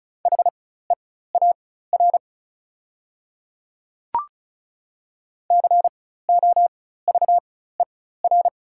Morse code, heard as a keyed tone, about 36 wpm, spaced as for 16 wpm, 690 Hz. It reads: HEAR COVER
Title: 4x Character Spacing - Top 500 Words - 35wpm